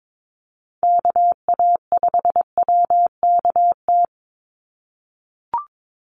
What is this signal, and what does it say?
Morse code, heard as a keyed tone, about 22 wpm, 700 Hz. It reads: KA5WKT
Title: USA Call Signs 22wpm